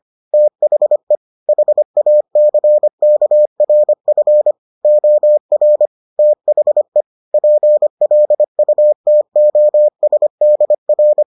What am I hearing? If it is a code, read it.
Morse code, heard as a keyed tone, about 25 wpm, 600 Hz. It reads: THE HACKRF OR THE PLUTOSDR